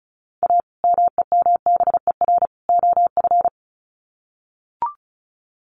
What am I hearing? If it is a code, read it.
Morse code, heard as a keyed tone, about 34 wpm, 700 Hz. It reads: A MEMBER OF
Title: Sets of 3 Words 35wpm